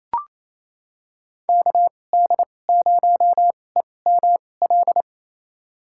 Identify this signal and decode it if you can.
Morse code, heard as a keyed tone, about 28 wpm, spaced as for 20 wpm, 700 Hz. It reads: KD0EML
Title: USA Call Signs - 6 Characters - 2x Character Spacing - 28wpm